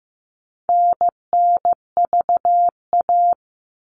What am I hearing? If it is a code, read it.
Morse code, heard as a keyed tone, about 15 wpm, 700 Hz. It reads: NNVA
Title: All Cut Numbers - 4-Digit 15wpm